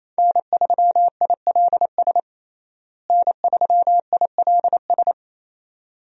Morse code, heard as a keyed tone, about 28 wpm, 700 Hz. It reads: N3ILS N3ILS
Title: USA Call Signs - 5 Characters - Repeated 2x - 28wpm